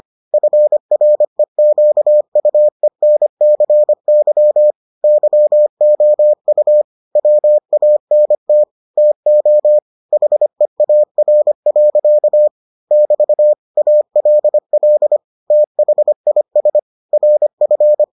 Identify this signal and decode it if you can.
Morse code, heard as a keyed tone, about 25 wpm, 600 Hz. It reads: FREQUENCY YOU WANT TO HEAR. <BT> ALL THIS RF